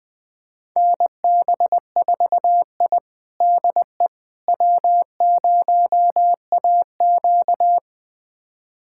Morse code, heard as a keyed tone, about 20 wpm, 700 Hz. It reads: NB4I DE W0AQ
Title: USA Callsign DE Callsign - 4 Characters - 20wpm